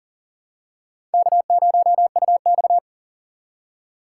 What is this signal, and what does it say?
Morse code, heard as a keyed tone, about 38 wpm, 700 Hz. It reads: K0UX
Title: USA Call Signs - 3x Character Spacing - 40wpm